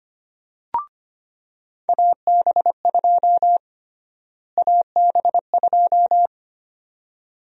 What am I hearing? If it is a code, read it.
Morse code, heard as a keyed tone, about 25 wpm, 700 Hz. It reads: AB2 AB2
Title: USA Call Sign Prefixes - Repeated 2x - 25wpm